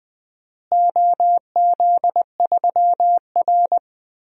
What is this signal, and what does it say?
Morse code, heard as a keyed tone, about 20 wpm, 700 Hz. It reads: OZ3R
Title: DXCC-100 Call Signs - 4 Characters - Repeated 2x - 20wpm